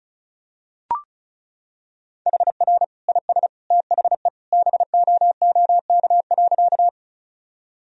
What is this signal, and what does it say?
Morse code, heard as a keyed tone, about 34 wpm, 700 Hz. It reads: HR IS THE BOOK.
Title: Sentences from Top 300 Words and Abbreviations 35wpm